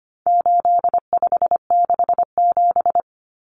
Morse code, heard as a keyed tone, about 25 wpm, 700 Hz. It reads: 8567
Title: Numbers - 4-Digit 25wpm